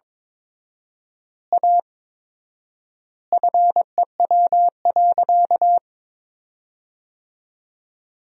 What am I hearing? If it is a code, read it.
Morse code, heard as a keyed tone, about 22 wpm, 700 Hz. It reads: A FEW.
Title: 4x Word-Spacing - Sentences from Top 300 Words - 22wpm